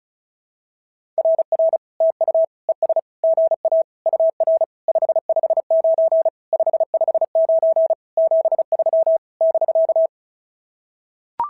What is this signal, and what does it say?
Morse code, heard as a keyed tone, about 34 wpm, 650 Hz. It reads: RR TU ES GA UR 559 559 73 <BK>
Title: SOTA QSO Practice 35wpm